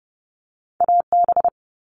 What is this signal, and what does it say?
Morse code, heard as a keyed tone, about 30 wpm, 700 Hz. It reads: AB
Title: Canadian Provinces and Territories - Repeated 3x - 30wpm